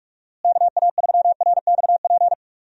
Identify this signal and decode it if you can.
Morse code, heard as a keyed tone, about 41 wpm, 700 Hz. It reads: KA3RXP
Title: USA Call Signs - 6 Characters - Repeated 3x - 45wpm